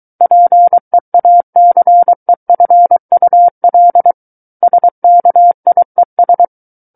Morse code, heard as a keyed tone, about 23 wpm, 700 Hz. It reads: PEACEFUL SKIES
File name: Book One / 14-Chapter 14 - 23 WPM